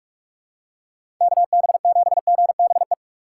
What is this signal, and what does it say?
Morse code, heard as a keyed tone, about 44 wpm, 700 Hz. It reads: KB7GBE
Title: USA Call Signs - 6 Characters - 6x Character Spacing - 45wpm